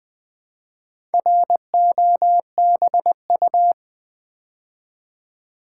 Morse code, heard as a keyed tone, about 20 wpm, 700 Hz. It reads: ROBU
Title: Build A Word - 2 Letters At A Time - 6-Letter Words - 20wpm